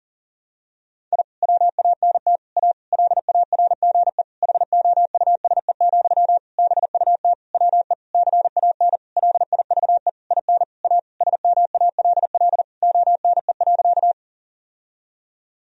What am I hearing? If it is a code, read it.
Morse code, heard as a keyed tone, about 40 wpm, 700 Hz. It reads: I WANT A LARGE HOUSE, BUT WE CAN LIVE IN A SMALL ONE.
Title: Sentences from Top 1000 Words - Review Part 1 - 40wpm